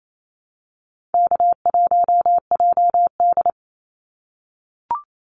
Morse code, heard as a keyed tone, about 28 wpm, 700 Hz. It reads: K1JD
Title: NA SOTA Activator Callsigns 28wpm